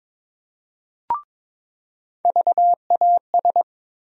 Morse code, heard as a keyed tone, about 22 wpm, 700 Hz. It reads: VAS
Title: USA Call Sign Postfixes - 3 Letters 22wpm